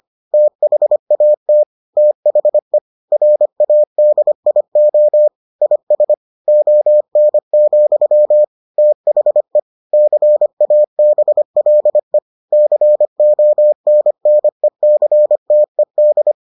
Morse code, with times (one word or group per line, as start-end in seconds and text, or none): 0.3-1.6 THAT
2.0-2.8 THE
3.1-5.3 RADIO
5.6-6.1 IS
6.5-8.4 ON,
8.8-9.6 THE
9.9-12.2 CABLE
12.5-16.3 CONNECTED